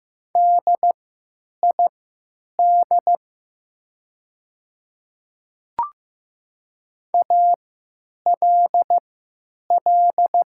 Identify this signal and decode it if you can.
Morse code, heard as a keyed tone, about 15 wpm, spaced as for 9 wpm, 700 Hz. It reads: DID ALL
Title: 3x Character Spacing - Top 100 Words - 15wpm